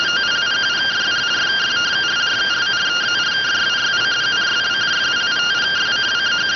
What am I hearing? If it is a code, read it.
Two-tone teleprinter signal, about 170 Hz shift.
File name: findings / MR CORONAL 55.gen-20 rx